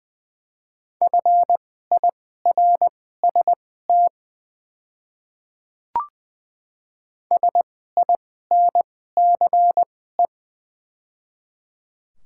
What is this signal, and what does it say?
Morse code, heard as a keyed tone, about 20 wpm, spaced as for 15 wpm, 700 Hz. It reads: FIRST SINCE
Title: 2x Character Spacing - Top 300 Words - 20wpm